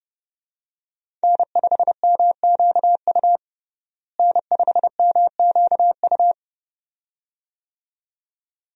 Morse code, heard as a keyed tone, about 32 wpm, 700 Hz. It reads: N5MQU N5MQU
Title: USA Call Signs - 5 Characters - Repeated 2x - 30wpm